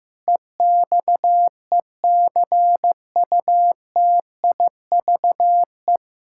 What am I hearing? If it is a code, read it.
Morse code, heard as a keyed tone, about 15 wpm, 700 Hz. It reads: EXECUTIVE
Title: K1USN SST QSO Practice 15wpm